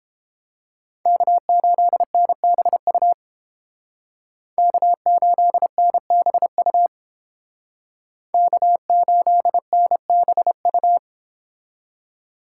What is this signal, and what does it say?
Morse code, heard as a keyed tone, about 29 wpm, 700 Hz. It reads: K8NBU K8NBU K8NBU